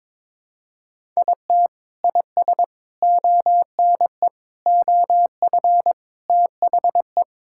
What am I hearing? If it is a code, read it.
Morse code, heard as a keyed tone, about 21 wpm, 700 Hz. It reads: IT IS ONE OF THE
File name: Build A Phrase - 5-Word Phrases - 22wpm